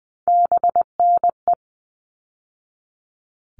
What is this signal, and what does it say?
Morse code, heard as a keyed tone, about 20 wpm, 700 Hz. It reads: BNE
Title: All Cut Numbers - 3-Digit 20wpm